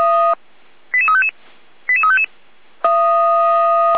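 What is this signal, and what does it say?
Five-tone selective call (ZVEI), heard as quick runs of five tones, tones of about 70 ms. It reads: 88355 88355
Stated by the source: ZVEI five-tone call, code 88247, received by SDR, from Feuerwehr Orlamünde.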